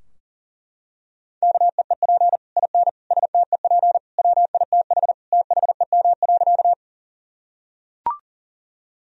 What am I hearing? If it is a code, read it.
Morse code, heard as a keyed tone, about 40 wpm, 700 Hz. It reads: KEEP IN STEP WITH THEM.
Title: Sentences from Top 1000 Words - Review Part 2 - 40wpm